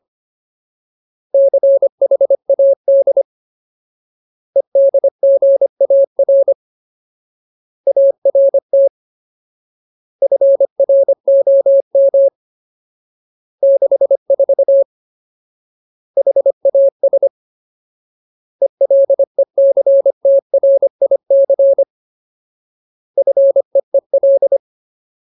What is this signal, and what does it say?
Morse code, heard as a keyed tone, about 25 wpm, 550 Hz. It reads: CHAD EDGAR ART FROM 64 HAS ELECTRIC FEEL
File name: Ultimate Word List / 25WPM Ultimate